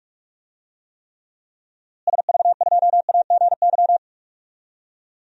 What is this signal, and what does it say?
Morse code, heard as a keyed tone, about 46 wpm, 700 Hz. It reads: SV2AGY
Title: DXCC-100 Call Signs - 6 Characters - Repeated 2x - 45wpm